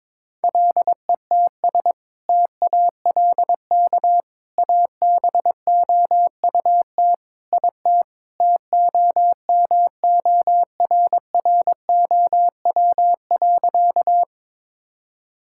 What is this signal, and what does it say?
Morse code, heard as a keyed tone, about 22 wpm, 700 Hz. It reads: LETS TALK ABOUT IT TOMORROW.
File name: practice / 22wpm / Sentences with 2-Word Phrases - 22wpm